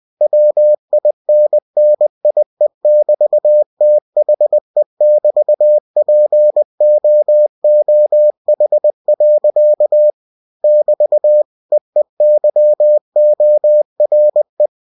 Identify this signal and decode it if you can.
Morse code, heard as a keyed tone, about 20 wpm, 600 Hz. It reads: WINNIE<BT>THE<BT>POOH. <BT> EEYORE
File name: cw-books / winnie-the-pooh / winnie-20wpm-page-0026